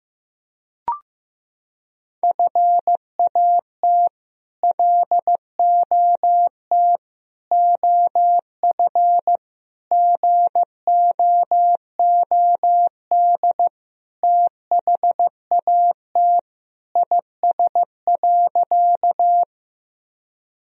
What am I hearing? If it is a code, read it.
Morse code, heard as a keyed tone, about 16 wpm, 700 Hz. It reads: FAT LOT OF GOOD THAT IS.
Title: Sentences from Top 1000 Words - Review Part 4 - 15wpm